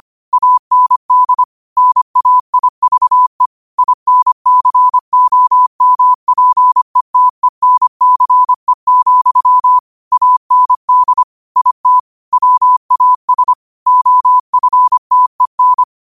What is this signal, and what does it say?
Morse code, heard as a keyed tone, about 25 wpm, 1000 Hz. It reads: AND NAIVE INCOMPETENCE, AND IT WAS OFTEN
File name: qotdfiles / 25wpm